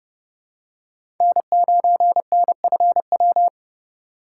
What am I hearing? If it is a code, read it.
Morse code, heard as a keyed tone, about 30 wpm, 700 Hz. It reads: N9NFW
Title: USA Call Signs - 5 Characters - 3x Character Spacing - 30wpm